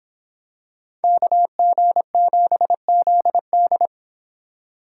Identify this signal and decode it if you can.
Morse code, heard as a keyed tone, about 25 wpm, 700 Hz. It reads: KG7ZD